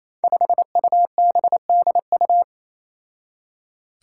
Morse code, heard as a keyed tone, about 28 wpm, 700 Hz. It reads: 5UBDU
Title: Five Challenging Letter-Number Combinations 28wpm